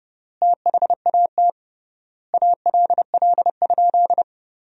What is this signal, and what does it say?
Morse code, heard as a keyed tone, about 31 wpm, 700 Hz. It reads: THAT ALL?
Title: 3x Word-Spacing - Sentences from Top 200 Words - 30wpm